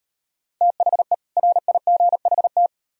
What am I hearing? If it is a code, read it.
Morse code, heard as a keyed tone, about 38 wpm, 700 Hz. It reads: THE RIGHT